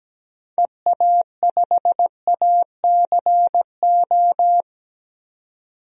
Morse code, heard as a keyed tone, about 17 wpm, 700 Hz. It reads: EA5ACO
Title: DXCC-100 Call Signs - 6 Characters - 17wpm